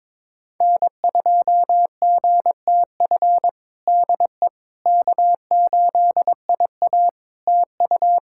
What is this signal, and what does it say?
Morse code, heard as a keyed tone, about 22 wpm, 700 Hz. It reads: N2GTF DE K8IA TU
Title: Practice QSO - Exchange 4 - Ending - 22wpm